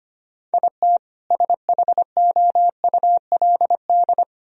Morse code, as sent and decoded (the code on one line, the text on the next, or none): .. - / ... .... --- ..- .-.. -..
IT SHOULD